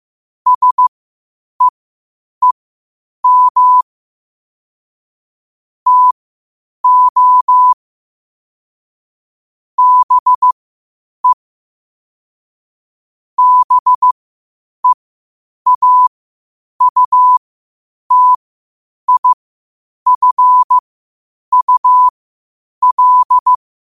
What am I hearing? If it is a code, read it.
Morse code, heard as a keyed tone, about 15 wpm, spaced as for 8 wpm, 1000 Hz. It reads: SEEM TO BE BEAUTIFUL
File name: qotdfiles / 8wpm